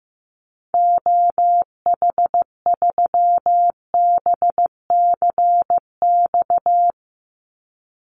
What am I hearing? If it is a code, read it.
Morse code, heard as a keyed tone, about 15 wpm, 700 Hz. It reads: OH3BCX
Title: Call Signs Repeated 2x - 15wpm